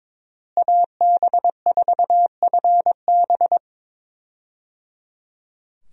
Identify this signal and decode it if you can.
Morse code, heard as a keyed tone, about 22 wpm, 700 Hz. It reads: AB4FB
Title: USA Call Signs - 5 Characters - Repeated 3x - 22wpm